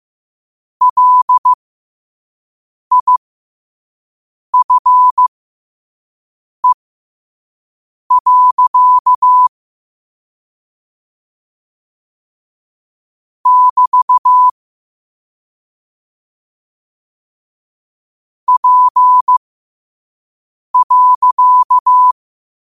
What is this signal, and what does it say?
Morse code, heard as a keyed tone, about 15 wpm, spaced as for 5 wpm, 1000 Hz. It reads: LIFE. <BT> P.
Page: qotd-at-5-wpm